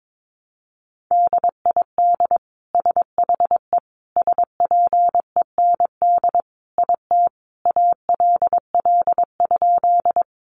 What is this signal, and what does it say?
Morse code, heard as a keyed tone, about 22 wpm, 700 Hz. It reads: DID SHE SPEND IT ALL?
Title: Sentences from Top 800 Words 22wpm